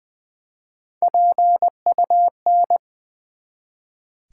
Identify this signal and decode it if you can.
Morse code, heard as a keyed tone, about 21 wpm, 700 Hz. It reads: PUN
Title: Build A Word - 1 Letter At A Time - 4-Letter Words - 20wpm